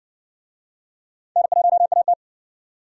Morse code, heard as a keyed tone, about 60 wpm, 700 Hz. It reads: N1AT